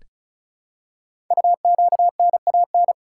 Morse code, heard as a keyed tone, about 36 wpm, 700 Hz. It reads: UQNAN
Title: Code Groups - 5 Letters - 35wpm